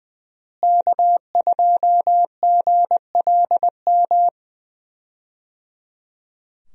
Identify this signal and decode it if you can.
Morse code, heard as a keyed tone, about 20 wpm, 700 Hz. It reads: K2GLM
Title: USA Call Signs - 5 Characters - Repeated 3x - 20wpm